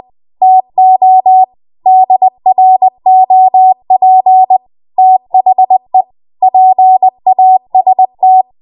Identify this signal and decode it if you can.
Morse code, heard as a keyed tone, about 20 wpm, 750 Hz. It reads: TO DROP THE PAST
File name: Morse / 20WPM